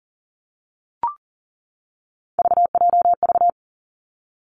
Morse code, heard as a keyed tone, about 40 wpm, 700 Hz. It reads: VJV
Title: Three Challenging Letter-Number Combinations 40wpm